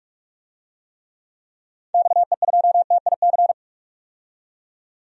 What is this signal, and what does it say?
Morse code, heard as a keyed tone, about 42 wpm, 700 Hz. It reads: XE2TIC
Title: DXCC-100 Call Signs - 6 Characters - 45wpm